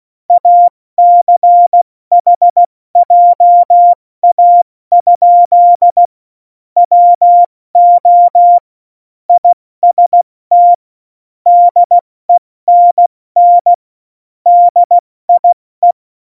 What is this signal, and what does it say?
Morse code, heard as a keyed tone, about 16 wpm, 700 Hz. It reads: ACH JA? WO IST DENN DIE
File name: Morsekurs / Witze / Witz 22. Oktober 2019, 70 F80